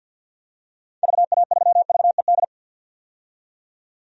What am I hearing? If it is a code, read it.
Morse code, heard as a keyed tone, about 48 wpm, 700 Hz. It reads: VA3VED
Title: DXCC-100 Call Signs - 6 Characters - 50wpm